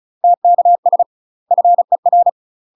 Morse code, heard as a keyed tone, about 34 wpm, 700 Hz. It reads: TKS FER
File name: UK MASTER QSOs TRAINING / 35WPM / UK MASTER QSOs 0042